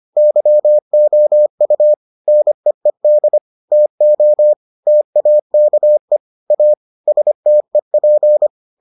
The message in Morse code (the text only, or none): YOU NEED TO TAKE A STEP